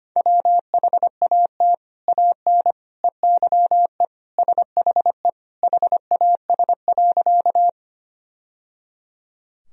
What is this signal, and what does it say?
Morse code, heard as a keyed tone, about 25 wpm, 700 Hz. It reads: WHAT AN EYE SHE HAS.
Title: Sentences from Top 1000 Words - Review Part 1 - 25wpm